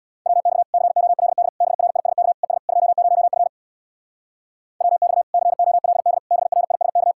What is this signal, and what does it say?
Morse code, heard as a keyed tone, about 62 wpm, 700 Hz. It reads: WX COLD BRISK ES 20F WX COLD BRISK
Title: Speed-Racing Common QSO Phrases - 50wpm